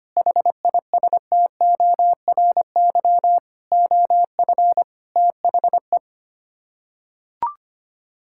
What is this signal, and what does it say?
Morse code, heard as a keyed tone, about 25 wpm, 700 Hz. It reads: HISTORY OF THE
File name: Sets of 9 Words - 25wpm